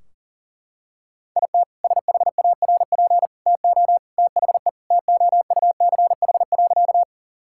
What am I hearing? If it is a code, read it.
Morse code, heard as a keyed tone, about 40 wpm, 700 Hz. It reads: IT SHARP TO THE TOUCH.
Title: Sentences from Top 1600 Words 40wpm